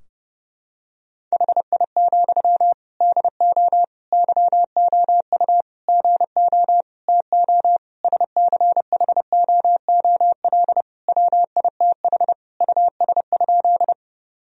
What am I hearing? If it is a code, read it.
Morse code, heard as a keyed tone, about 32 wpm, 700 Hz. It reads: HI, DO YOU GO TO SCHOOL WITH US?
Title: Sentences from Top 1000 Words - Review Part 3 - 30wpm